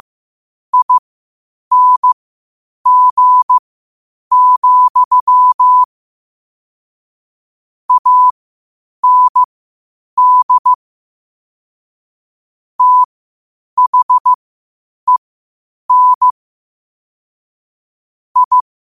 Morse code, heard as a keyed tone, about 15 wpm, spaced as for 8 wpm, 1000 Hz. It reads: ING, AND THEN I
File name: qotdfiles / 8wpm